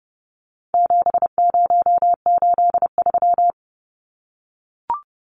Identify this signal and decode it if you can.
Morse code, heard as a keyed tone, about 30 wpm, 700 Hz. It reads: 7083
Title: Numbers - 4-Digit 30wpm